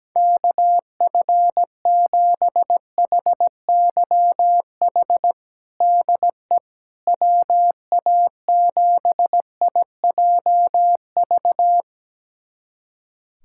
Morse code, heard as a keyed tone, about 17 wpm, 700 Hz. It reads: KF7HYH DE WA7IJV
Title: USA Callsign DE Callsign - 6 Characters - 17wpm